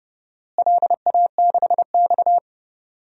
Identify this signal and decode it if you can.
Morse code, heard as a keyed tone, about 32 wpm, 700 Hz. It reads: LA6X
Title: DXCC-100 Call Signs - 4 Characters - Repeated 2x - 30wpm